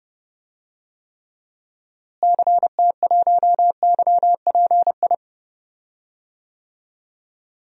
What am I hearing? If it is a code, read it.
Morse code, heard as a keyed tone, about 30 wpm, 700 Hz. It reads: CT1YPI
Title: DXCC-100 Call Signs - 6 Characters - Repeated 3x - 30wpm